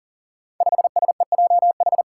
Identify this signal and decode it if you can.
Morse code, heard as a keyed tone, about 37 wpm, 700 Hz. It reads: 5SEJH
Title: Five Challenging Letter-Number Combinations 40wpm